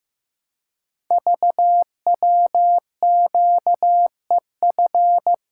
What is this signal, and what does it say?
Morse code, heard as a keyed tone, about 15 wpm, 700 Hz. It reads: VWQEF
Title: Code Groups - 5 Letters - 15wpm